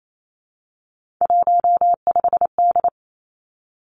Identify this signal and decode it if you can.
Morse code, heard as a keyed tone, about 28 wpm, 700 Hz. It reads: 15D
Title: Three Challenging Letter-Number Combinations 28wpm